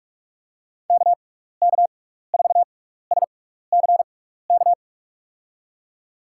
Morse code, heard as a keyed tone, about 42 wpm, spaced as for 15 wpm, 700 Hz. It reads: KK4SCK
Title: USA Call Signs - 6 Characters - 6x Character Spacing - 45wpm